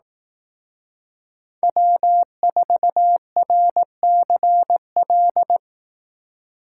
Morse code, heard as a keyed tone, about 18 wpm, 700 Hz. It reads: W4RCL
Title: Speed Racing - USA 1x3 Call Signs 15wpm